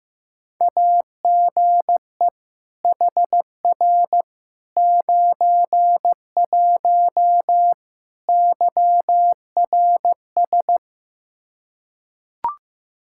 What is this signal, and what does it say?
Morse code, heard as a keyed tone, about 16 wpm, 700 Hz. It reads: AGE HR 91 YRS
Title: K1USN SST QSO Practice 15wpm